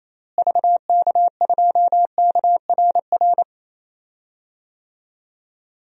Morse code, heard as a keyed tone, about 28 wpm, 700 Hz. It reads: VK2KRR
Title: DXCC-100 Call Signs - 6 Characters - Repeated 2x - 28wpm